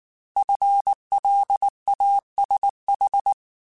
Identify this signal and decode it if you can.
Morse code, heard as a keyed tone, about 19 wpm, 800 Hz. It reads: FLASH
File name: Alphabetical / Alphabetical mp3s / Fs @18